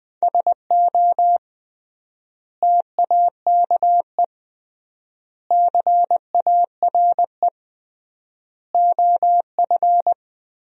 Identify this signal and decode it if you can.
Morse code, heard as a keyed tone, about 20 wpm, 700 Hz. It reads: SO TAKE CARE OF
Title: 3x Word-Spacing - Sentences from Top 300 Words - 20wpm